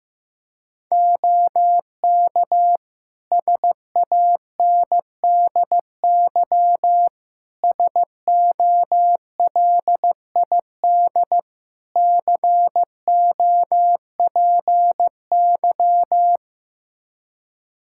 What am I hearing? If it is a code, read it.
Morse code, heard as a keyed tone, about 15 wpm, 700 Hz. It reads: OK SANDY SOLID COPY
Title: Practice QSO - Exchange 3 - More Info - 15wpm